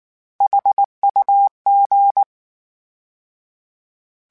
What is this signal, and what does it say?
Morse code, heard as a keyed tone, about 19 wpm, 800 Hz. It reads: HUG
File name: H Prefixes / H mp3s